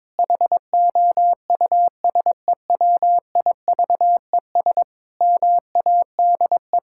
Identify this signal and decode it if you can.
Morse code, heard as a keyed tone, about 22 wpm, 700 Hz. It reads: HOUSEWIVES MADE
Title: Pangram Challenge v2 - 22wpm